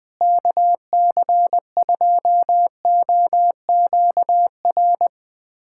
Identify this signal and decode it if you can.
Morse code, heard as a keyed tone, about 21 wpm, 700 Hz. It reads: KC2OQR
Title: USA Callsign DE Callsign 20wpm